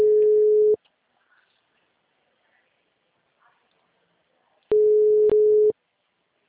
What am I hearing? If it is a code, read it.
Ringback tone (Europe, 425 Hz).